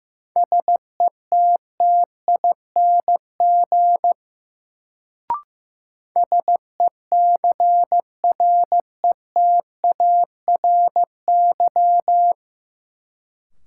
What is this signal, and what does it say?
Morse code, heard as a keyed tone, about 15 wpm, 700 Hz. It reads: SETTING SECRETARY
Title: Sentences from Top 1300 Words - New Word Drill Alt - 15wpm